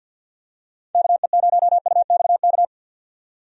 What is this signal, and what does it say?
Morse code, heard as a keyed tone, about 48 wpm, 700 Hz. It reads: KE0UXK
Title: USA Call Signs - 6 Characters - 50wpm